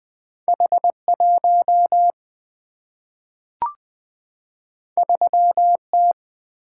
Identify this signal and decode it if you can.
Morse code, heard as a keyed tone, about 20 wpm, 700 Hz. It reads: H1 3T
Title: Single-Letter Single-Number 20wpm